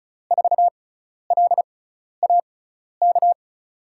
Morse code, heard as a keyed tone, about 34 wpm, spaced as for 12 wpm, 700 Hz. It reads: 4LAK
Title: USA Call Signs - 5 Characters - 6x Character Spacing - 35wpm